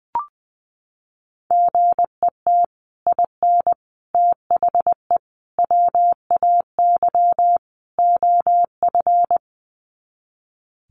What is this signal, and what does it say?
Morse code, heard as a keyed tone, about 19 wpm, 700 Hz. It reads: GET IN THE WAY OF
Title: Sets of 5 Words Encore 20wpm